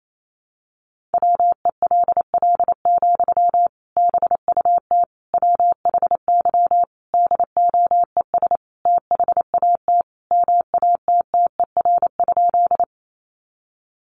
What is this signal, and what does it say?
Morse code, heard as a keyed tone, about 28 wpm, 700 Hz. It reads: WELL, BUT WHY DOES THAT MATTER?
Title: Sentences from Top 700 Words 28wpm